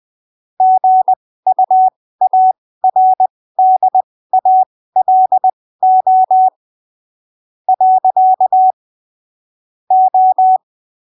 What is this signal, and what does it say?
Morse code, heard as a keyed tone, about 20 wpm, spaced as for 15 wpm, 750 Hz. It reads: GUARDALO . O
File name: arquivos / texto / texto 20 20ppm